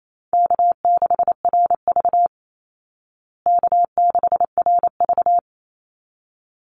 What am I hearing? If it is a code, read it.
Morse code, heard as a keyed tone, about 28 wpm, 700 Hz. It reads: K6RV K6RV
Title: Easy Call Signs Repeated 2x - 28wpm